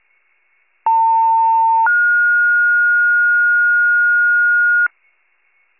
Two-tone page, heard as a one-second tone, then a three-second tone, about 900 then 1430 Hz, with noise. It is heard from 0.9 to 4.9 s.